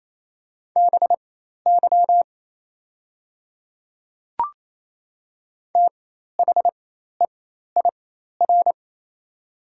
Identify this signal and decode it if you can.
Morse code, heard as a keyed tone, about 27 wpm, spaced as for 13 wpm, 690 Hz. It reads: BY THEIR
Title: 4x Character Spacing - Top 100 Words - 28wpm